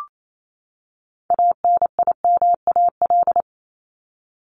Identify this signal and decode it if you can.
Morse code, heard as a keyed tone, about 28 wpm, 700 Hz. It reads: ANIMAL
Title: ES:S Suffix - Top 100 Words with Base Words 28wpm